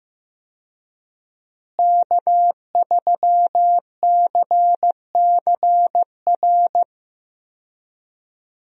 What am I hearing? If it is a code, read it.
Morse code, heard as a keyed tone, about 15 wpm, 700 Hz. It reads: K3CCR
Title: USA Call Signs - 5 Characters - Repeated 3x - 15wpm